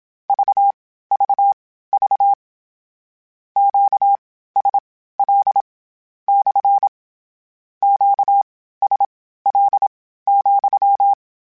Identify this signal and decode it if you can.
Morse code, heard as a keyed tone, about 27 wpm, spaced as for 15 wpm, 800 Hz. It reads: VVV QSL / QSL,
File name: Q Calls / Q Calls-mp3s / Q CALLS SPSL @25